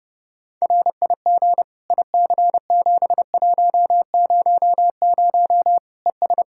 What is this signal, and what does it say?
Morse code, heard as a keyed tone, about 30 wpm, 700 Hz. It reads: RIG IC7100 ES
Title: K1USN SST QSO Practice 30wpm